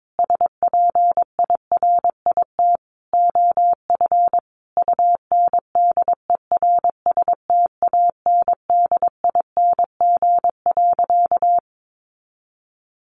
Morse code, heard as a keyed tone, about 22 wpm, 700 Hz. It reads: SPIRIT OF UNDERSTANDING.
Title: Sentences from Top 1400 Words 22wpm